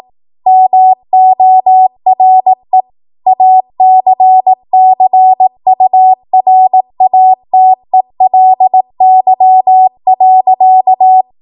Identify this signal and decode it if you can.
Morse code, heard as a keyed tone, about 18 wpm, 750 Hz. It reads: MORE ACCURATELY.
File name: Morse / 18WPM